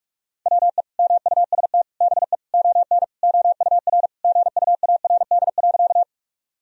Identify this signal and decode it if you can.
Morse code, heard as a keyed tone, about 42 wpm, 700 Hz. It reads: WE MUST BE ON OUR GUARD.
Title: Sentences from Top 1000 Words - Review Part 4 - 45wpm